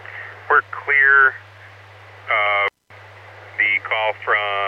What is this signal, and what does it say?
Emergency Alert System (SAME) tones, heard as harsh buzzing bursts, with speech. It is heard from 0.8 to 3.9 s.